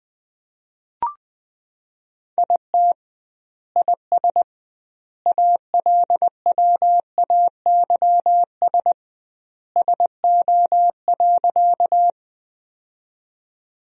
Morse code, heard as a keyed tone, about 20 wpm, 700 Hz. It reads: IT IS ALWAYS SO.
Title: 2x Word-Spacing - Sentences from Top 300 Words - 20wpm